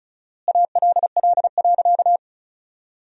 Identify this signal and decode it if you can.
Morse code, heard as a keyed tone, about 34 wpm, 700 Hz. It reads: ALL.
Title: 4x Word-Spacing - Sentences from Top 300 Words - 35wpm